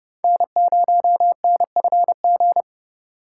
Morse code, heard as a keyed tone, about 30 wpm, 700 Hz. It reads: N0NFG
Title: USA Call Signs - 5 Characters - 4x Character Spacing - 30wpm